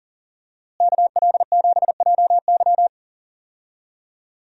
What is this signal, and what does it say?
Morse code, heard as a keyed tone, about 40 wpm, 700 Hz. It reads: KL7JY
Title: DXCC-100 Call Signs - 5 Characters - Repeated 2x - 40wpm